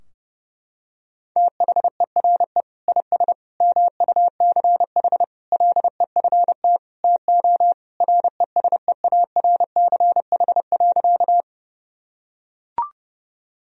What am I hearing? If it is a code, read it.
Morse code, heard as a keyed tone, about 30 wpm, 700 Hz. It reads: THERE IS MUCH LEFT TO RESEARCH.
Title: Sentences from Top 1000 Words - Review Part 4 - 30wpm